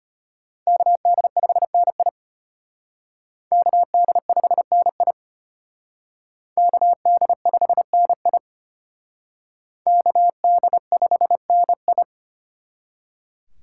Morse code, heard as a keyed tone, about 33 wpm, 700 Hz. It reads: KD5NI KD5NI KD5NI KD5NI